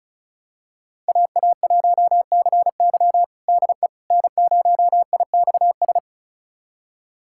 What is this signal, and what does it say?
Morse code, heard as a keyed tone, about 36 wpm, 700 Hz. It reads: AA1CY DE N0IXS
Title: USA Callsign DE Callsign - 5 Characters - 35wpm